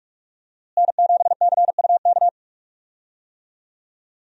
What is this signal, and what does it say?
Morse code, heard as a keyed tone, about 42 wpm, 700 Hz. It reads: N7CUK
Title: USA Call Signs - 5 Characters - Repeated 3x - 45wpm